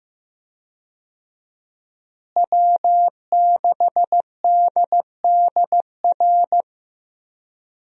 Morse code, heard as a keyed tone, about 15 wpm, 700 Hz. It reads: W6DDR
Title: USA Call Signs - 5 Characters - Repeated 3x - 15wpm